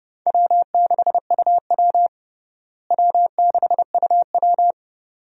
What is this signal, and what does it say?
Morse code, heard as a keyed tone, about 29 wpm, 700 Hz. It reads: W6UW W6UW
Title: USA Call Signs - 4 Characters - Repeated 2x - 30wpm